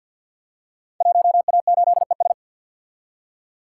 Morse code, heard as a keyed tone, about 51 wpm, 700 Hz. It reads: 1A8ES
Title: Code Groups - 5 Letter:Numbers - 50wpm